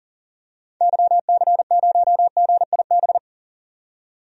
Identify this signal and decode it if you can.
Morse code, heard as a keyed tone, about 40 wpm, 700 Hz. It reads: YC0GIB
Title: DXCC-100 Call Signs - 6 Characters - 40wpm